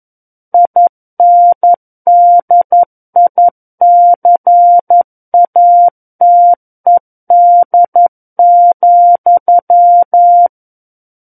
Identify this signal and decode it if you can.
Morse code, heard as a keyed tone, about 11 wpm, 700 Hz. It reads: INDICATED,
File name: Book One / 02-Chapter 2 - 11 WPM